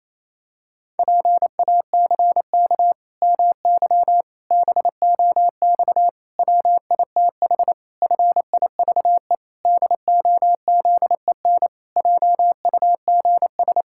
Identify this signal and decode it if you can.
Morse code, heard as a keyed tone, about 28 wpm, 700 Hz. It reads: PACK MY BOX WITH FIVE DOZEN JUGS
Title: Pangram Challenge v2 - 28wpm